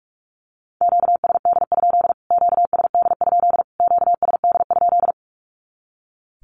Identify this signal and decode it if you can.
Morse code, heard as a keyed tone, about 44 wpm, 700 Hz. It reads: QSD? QSD? QSD?
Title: Q-Codes Repeated 3x 45wpm